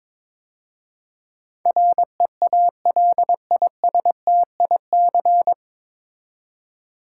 Morse code, heard as a keyed tone, about 22 wpm, 690 Hz. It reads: REALISTIC
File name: 9 Letter Words - First 3 Letters Given - 22wpm